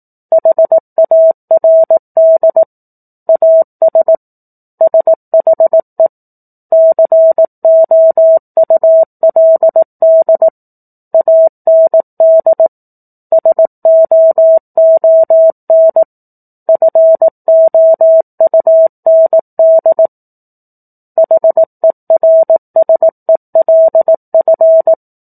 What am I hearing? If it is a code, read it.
Morse code, heard as a keyed tone, about 18 wpm, 650 Hz. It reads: HARD AS SHE COULD AND SOON FOUND HERSELF
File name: 04-Chapter-04 18 WPM